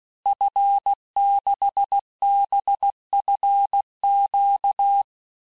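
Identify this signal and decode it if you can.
Morse code, heard as a keyed tone, about 16 wpm, 800 Hz. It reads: F6BFQ
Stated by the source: Morse course text, end of QSO, sent at 15 words per minute.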